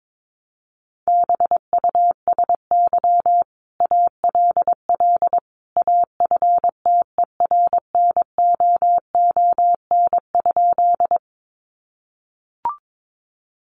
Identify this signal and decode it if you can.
Morse code, heard as a keyed tone, about 22 wpm, 700 Hz. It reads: BUSY ALL AFTERNOON?
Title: Sentences from Top 1500 Words 22wpm